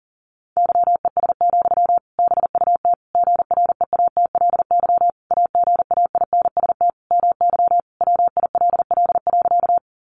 Morse code, heard as a keyed tone, about 40 wpm, 700 Hz. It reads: YES, BUT GREATLY AGAINST MY WILL.
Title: Sentences from Top 1400 Words 40wpm